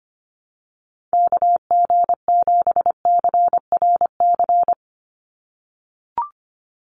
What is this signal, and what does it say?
Morse code, heard as a keyed tone, about 25 wpm, 700 Hz. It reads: KG7CRC
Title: USA Call Signs 25wpm